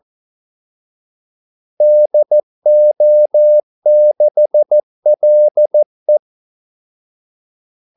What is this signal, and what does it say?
Morse code, heard as a keyed tone, about 14 wpm, 600 Hz. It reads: DO6LE
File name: Callsigns-14wpm